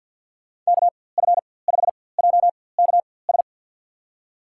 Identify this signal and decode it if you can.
Morse code, heard as a keyed tone, about 47 wpm, spaced as for 23 wpm, 700 Hz. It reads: KF5JKS
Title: USA Call Signs - 4x Character Spacing - 50wpm